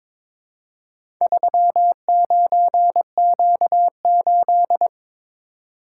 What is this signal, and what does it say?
Morse code, heard as a keyed tone, about 22 wpm, 700 Hz. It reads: 39Q8
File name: practice / 22wpm / Four Letter-Number Combinations - 22wpm